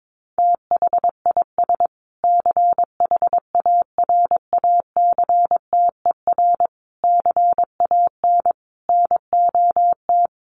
Morse code, heard as a keyed tone, about 22 wpm, 700 Hz. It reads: THIS CHARACTER CAN NOT
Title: Sets of 5 Words Encore2 22wpm